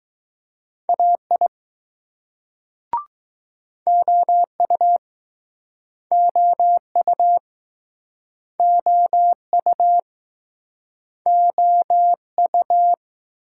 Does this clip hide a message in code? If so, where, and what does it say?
Yes: Morse code from 0.9 to 12.9 s, heard as a keyed tone, about 19 wpm, 700 Hz. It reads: AI OU OU OU OU